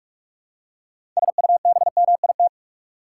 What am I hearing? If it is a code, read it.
Morse code, heard as a keyed tone, about 46 wpm, 700 Hz. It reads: SUBMIT
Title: Six Letter Words - 45wpm